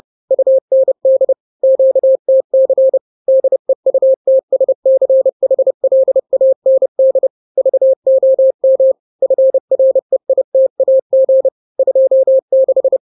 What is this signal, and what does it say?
Morse code, heard as a keyed tone, about 30 wpm, 530 Hz. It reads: UND QTC DEUTSCHLAND VOM FREITAG 26
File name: daily / 29wpm